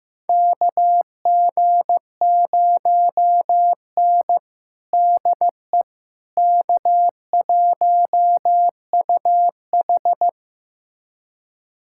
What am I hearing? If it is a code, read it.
Morse code, heard as a keyed tone, about 15 wpm, 700 Hz. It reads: KG0N DE K1UH
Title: USA Callsign DE Callsign - 4 Characters - 15wpm